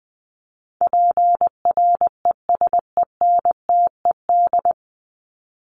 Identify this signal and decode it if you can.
Morse code, heard as a keyed tone, about 20 wpm, 700 Hz. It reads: PRESENTED
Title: ED Suffix - Top 100 Words with Base Words 20wpm